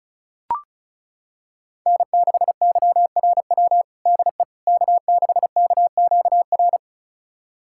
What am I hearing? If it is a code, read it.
Morse code, heard as a keyed tone, about 34 wpm, 700 Hz. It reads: N6YRW DE K6KQR
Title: USA Callsign DE Callsign - 5 Characters - 35wpm